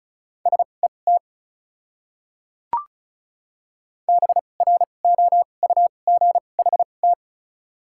Morse code, heard as a keyed tone, about 36 wpm, spaced as for 25 wpm, 700 Hz. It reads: SET BROUGHT
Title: 2x Character Spacing - Top 500 Words - 35wpm